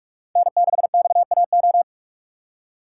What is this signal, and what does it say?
Morse code, heard as a keyed tone, about 46 wpm, 700 Hz. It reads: N6XAO
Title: USA Call Signs - 5 Characters - 4x Character Spacing - 45wpm